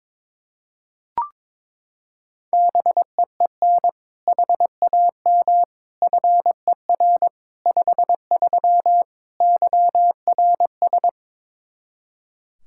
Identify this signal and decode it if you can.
Morse code, heard as a keyed tone, about 22 wpm, 700 Hz. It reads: BEEN HAM FER 53 YRS
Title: Practice QSO - Exchange 3 - More Info - 22wpm